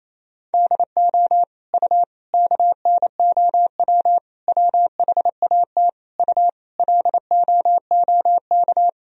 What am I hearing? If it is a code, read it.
Morse code, heard as a keyed tone, about 28 wpm, 700 Hz. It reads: DO U KNOW WHAT U LOOK
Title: Sentences from Top 100 Words and Abbreviations 28wpm